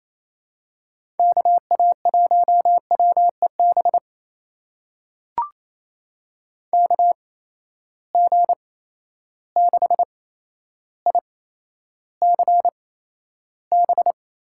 Morse code, heard as a keyed tone, about 27 wpm, spaced as for 12 wpm, 700 Hz. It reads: KA1WEB KG6ICB
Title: USA Call Signs - 6 Characters - 8x Character Spacing - 28wpm